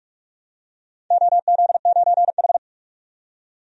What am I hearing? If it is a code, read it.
Morse code, heard as a keyed tone, about 42 wpm, 700 Hz. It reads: OZ9H
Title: DXCC-100 Call Signs - 4 Characters - 45wpm